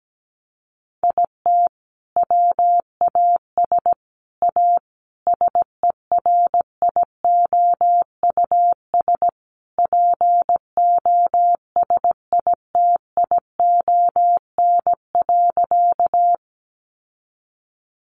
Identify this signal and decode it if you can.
Morse code, heard as a keyed tone, about 17 wpm, 700 Hz. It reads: IT WAS A SERIOUS POSITION.
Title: Sentences from Top 1100 Words 17wpm